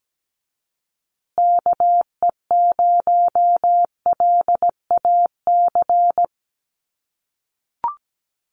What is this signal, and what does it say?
Morse code, heard as a keyed tone, about 17 wpm, 700 Hz. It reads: KE0LAC
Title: USA Call Signs Repeated 3x 17wpm